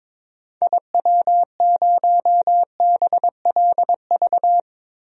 Morse code, heard as a keyed tone, about 22 wpm, 700 Hz. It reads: IW0BLV
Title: DXCC-100 Call Signs - 6 Characters - Repeated 2x - 22wpm